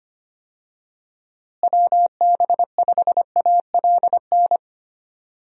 Morse code, heard as a keyed tone, about 25 wpm, 700 Hz. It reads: WB5ALN
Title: USA Call Signs - 6 Characters - Repeated 2x - 25wpm